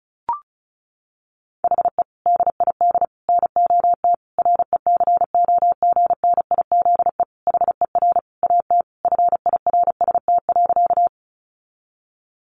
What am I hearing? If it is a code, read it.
Morse code, heard as a keyed tone, about 34 wpm, 700 Hz. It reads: HE DID NOT RECOGNIZE HER AT FIRST.
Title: Sentences from Top 1400 Words 35wpm